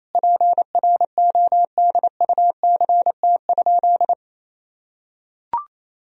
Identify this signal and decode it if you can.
Morse code, heard as a keyed tone, about 28 wpm, 700 Hz. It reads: PRODUCT?
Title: Sentences from Top 1000 Words - Review Part 2 - 28wpm